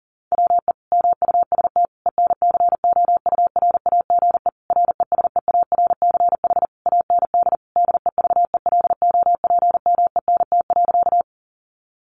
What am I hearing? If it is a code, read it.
Morse code, heard as a keyed tone, about 37 wpm, 700 Hz. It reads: WE MUST ENCOURAGE RESEARCH AND DEVELOPMENT.
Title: Sentences from Top 1400 Words 40wpm